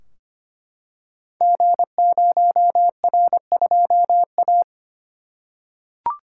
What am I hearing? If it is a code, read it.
Morse code, heard as a keyed tone, about 25 wpm, 700 Hz. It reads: G0R2A
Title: Code Groups - 5 Letter:Numbers - 25wpm